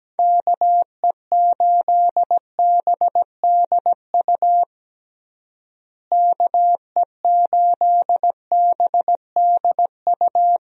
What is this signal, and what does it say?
Morse code, heard as a keyed tone, about 17 wpm, 700 Hz. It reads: KE8BDU KE8BDU
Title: USA Call Signs - 6 Characters - Repeated 3x - 17wpm